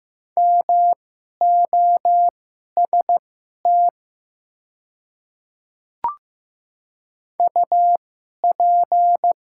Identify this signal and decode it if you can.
Morse code, heard as a keyed tone, about 15 wpm, spaced as for 11 wpm, 700 Hz. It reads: MOST UP
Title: 2x Character Spacing - Top 100 Words - 15wpm